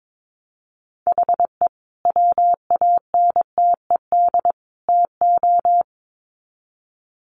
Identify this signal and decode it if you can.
Morse code, heard as a keyed tone, about 22 wpm, 700 Hz. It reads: HE WANTED TO
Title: Sets of 3 Words 22wpm